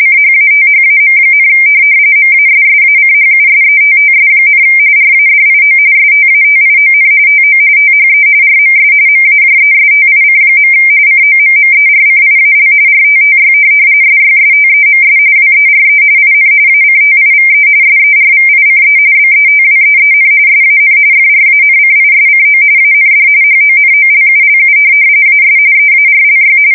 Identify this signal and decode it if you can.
RTTY, about 45.45 baud. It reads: CHATHAM, MASSACHUSETTS, RADIO STATION TO THE RMS MAJESTIC. COMMERCIAL RTTY SYSTEMS WERE IN ACTIVE SERVICE BETWEEN SAN FRANCISCO AND HONOLULU AS EARLY
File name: fsk170 / teletype-RTTY